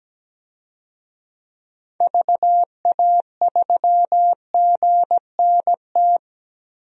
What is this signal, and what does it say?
Morse code, heard as a keyed tone, about 17 wpm, 700 Hz. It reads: VA3GNT
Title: DXCC-100 Call Signs - 6 Characters - 17wpm